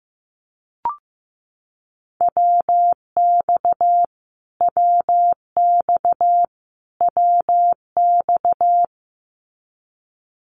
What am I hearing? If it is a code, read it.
Morse code, heard as a keyed tone, about 15 wpm, 700 Hz. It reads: WX WX WX
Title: Abbreviations Repeated 3x - 15wpm